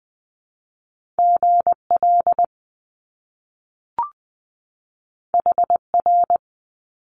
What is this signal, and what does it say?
Morse code, heard as a keyed tone, about 20 wpm, 700 Hz. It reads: GL HR
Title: Top Two Letter Combinations in Words 20wpm